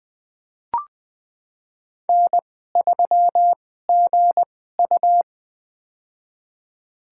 Morse code, heard as a keyed tone, about 20 wpm, spaced as for 14 wpm, 700 Hz. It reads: N3GU
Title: USA Call Signs - 4 Characters - 2x Character Spacing - 20wpm